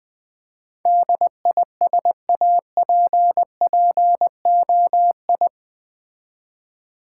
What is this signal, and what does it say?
Morse code, heard as a keyed tone, about 20 wpm, 700 Hz. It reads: DISAPPOI
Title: Build A Word - 4 Letters At A Time - 12-Letter Words - 20wpm